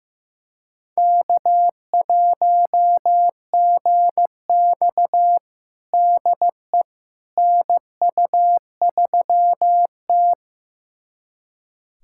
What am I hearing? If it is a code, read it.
Morse code, heard as a keyed tone, about 16 wpm, 700 Hz. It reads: K1GX DE NU3T
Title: USA Callsign DE Callsign - 4 Characters - 15wpm